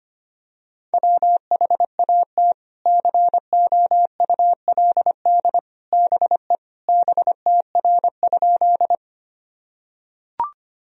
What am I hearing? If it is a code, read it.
Morse code, heard as a keyed tone, about 25 wpm, 700 Hz. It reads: WHAT COULD BE BTR?
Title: Sentences from Top 400 Words and Abbreviations 25wpm